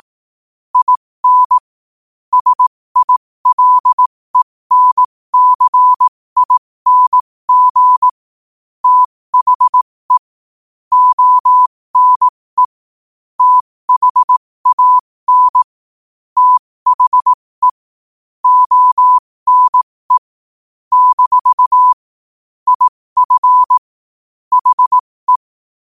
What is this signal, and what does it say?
Morse code, heard as a keyed tone, about 18 wpm, spaced as for 15 wpm, 1000 Hz. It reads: IN SILENCING THE ONE THAN THE ONE - IF HE
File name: qotdfiles / 15wpm